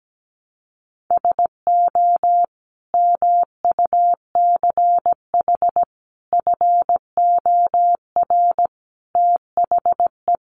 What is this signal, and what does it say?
Morse code, heard as a keyed tone, about 17 wpm, 700 Hz. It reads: SO MUCH FOR THE
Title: Sentences from Top 1000 Words 17wpm